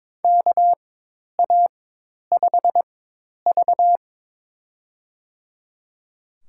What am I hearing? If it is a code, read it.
Morse code, heard as a keyed tone, about 22 wpm, spaced as for 10 wpm, 700 Hz. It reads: KA5V
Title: USA Call Signs - 4 Characters - 4x Character Spacing - 22wpm